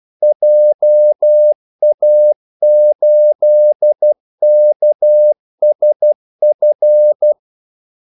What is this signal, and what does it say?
Morse code, heard as a keyed tone, about 12 wpm, 600 Hz. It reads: JA8KSF
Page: 2 ekstra ordmellemrum mellem kaldesignalerne